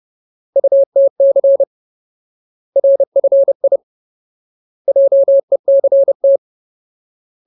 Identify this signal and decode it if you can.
Morse code, heard as a keyed tone, about 30 wpm, 550 Hz. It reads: UTC RFI JECT
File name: Ultimate Word List / 30WPM Ultimate